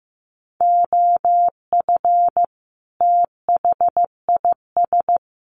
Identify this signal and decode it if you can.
Morse code, heard as a keyed tone, about 15 wpm, 700 Hz. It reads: OF THIS
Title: Sets of 5 Words Encore2 15wpm